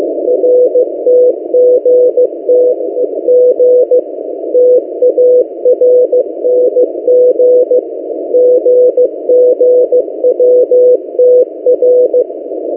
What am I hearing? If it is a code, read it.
Morse code, heard as a keyed tone, about 15 wpm, 500 Hz. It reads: RTGTG TARNG GGWTR
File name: M08a- -1355utc-8096khz